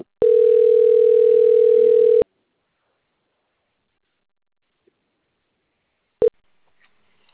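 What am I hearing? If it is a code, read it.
Ringback tone (North America), 440 + 480 Hz.
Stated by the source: Outgoing telephone call recording, Sao Carlos, Brazil.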